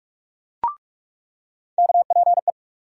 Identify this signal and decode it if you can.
Morse code, heard as a keyed tone, about 41 wpm, 700 Hz. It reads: KPE
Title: USA Call Sign Postfixes - 3 Letters 45wpm